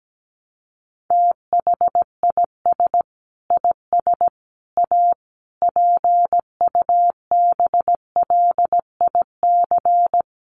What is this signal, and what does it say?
Morse code, heard as a keyed tone, about 17 wpm, 700 Hz. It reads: THIS IS A PUBLIC
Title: Sentences from Top 600 Words 17wpm